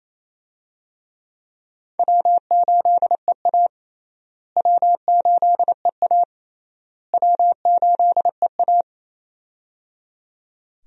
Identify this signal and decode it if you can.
Morse code, heard as a keyed tone, about 28 wpm, 700 Hz. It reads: W8EA W8EA W8EA
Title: USA Call Signs - 4 Characters - Repeated 3x - 28wpm